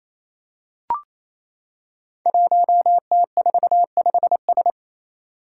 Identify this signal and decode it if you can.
Morse code, heard as a keyed tone, about 27 wpm, 700 Hz. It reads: 1T45S
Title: Five Challenging Letter-Number Combinations 28wpm